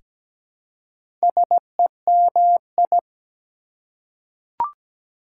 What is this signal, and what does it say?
Morse code, heard as a keyed tone, about 17 wpm, 700 Hz. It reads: SEMI
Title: Prefixes and Suffixes 17wpm